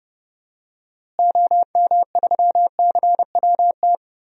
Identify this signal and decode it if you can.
Morse code, heard as a keyed tone, about 30 wpm, 700 Hz. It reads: OM3CWT
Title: DXCC-100 Call Signs - 6 Characters - Repeated 2x - 30wpm